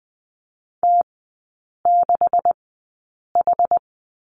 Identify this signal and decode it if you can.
Morse code, heard as a keyed tone, about 20 wpm, spaced as for 8 wpm, 700 Hz. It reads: T6H
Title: 127 Lesson 21e - Number 6 - Three Characters 20wpm